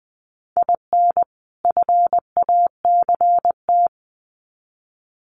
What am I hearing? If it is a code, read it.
Morse code, heard as a keyed tone, about 20 wpm, 700 Hz. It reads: IN FACT
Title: Sets of 2 Words 20wpm